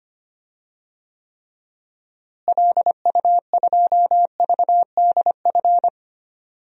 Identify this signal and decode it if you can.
Morse code, heard as a keyed tone, about 25 wpm, 700 Hz. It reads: LU2VDF
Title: DXCC-100 Call Signs - 6 Characters - Repeated 3x - 25wpm